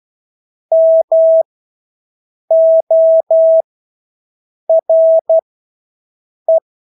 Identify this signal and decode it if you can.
Morse code, heard as a keyed tone, about 12 wpm, spaced as for 6 wpm, 650 Hz. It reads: MORE
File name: Engelske ord 12-6 wpm